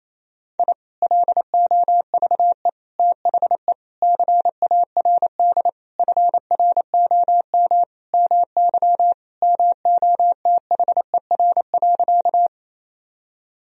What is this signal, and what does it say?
Morse code, heard as a keyed tone, about 28 wpm, 700 Hz. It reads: I LOVE THE CARD FROM MY MOTHER.
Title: Sentences from Top 1000 Words - Review Part 4 - 28wpm